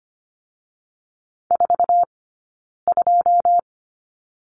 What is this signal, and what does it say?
Morse code, heard as a keyed tone, about 25 wpm, 700 Hz. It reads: 4 2
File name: course / 25:10wpm / 099 Lesson 17d - Letter u - Two Characters 25wpm